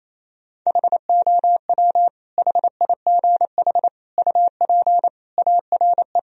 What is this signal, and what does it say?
Morse code, heard as a keyed tone, about 28 wpm, 700 Hz. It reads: HOW HIGH UP ARE
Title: Sentences from Top 1000 Words - Review Part 1 - 28wpm